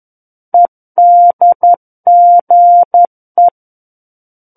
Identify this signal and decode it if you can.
Morse code, heard as a keyed tone, about 11 wpm, 700 Hz. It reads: EDGE
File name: 11 WPM